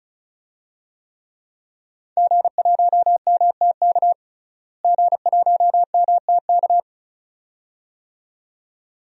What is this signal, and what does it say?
Morse code, heard as a keyed tone, about 35 wpm, 700 Hz. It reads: G1MTK G1MTK
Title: DXCC-100 Call Signs - 5 Characters - Repeated 2x - 35wpm